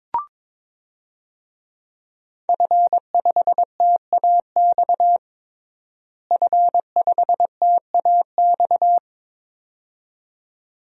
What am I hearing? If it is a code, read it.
Morse code, heard as a keyed tone, about 21 wpm, 700 Hz. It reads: F5TAX F5TAX
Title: DXCC-100 Call Signs - 5 Characters - Repeated 2x - 22wpm